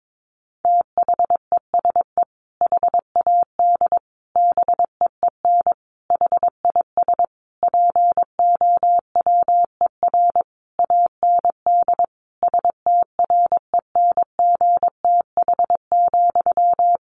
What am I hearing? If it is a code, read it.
Morse code, heard as a keyed tone, about 22 wpm, 700 Hz. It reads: THESE HAD BEEN HIS POWER AND STRENGTH,
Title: Sentences from Top 1500 Words 22wpm